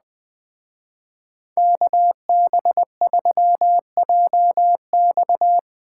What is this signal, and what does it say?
Morse code, heard as a keyed tone, about 20 wpm, 700 Hz. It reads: KB3JX